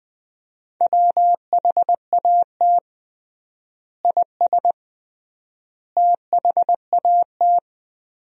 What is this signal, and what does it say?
Morse code, heard as a keyed tone, about 20 wpm, 700 Hz. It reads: WHAT IS THAT
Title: 3x Word-Spacing - Sentences from Top 200 Words - 20wpm